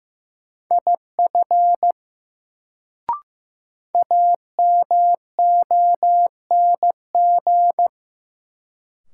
Morse code, heard as a keyed tone, about 16 wpm, 700 Hz. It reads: IF AMONG
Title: Sets of 2 Words - New Word Drill - 15wpm